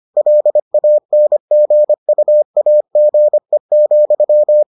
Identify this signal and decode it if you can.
Morse code, heard as a keyed tone, about 25 wpm, 600 Hz. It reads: LANGUAGE,